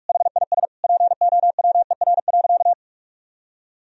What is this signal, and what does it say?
Morse code, heard as a keyed tone, about 46 wpm, 700 Hz. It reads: HIS POWER.
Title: Sentences from Top 1000 Words - Review Part 4 - 45wpm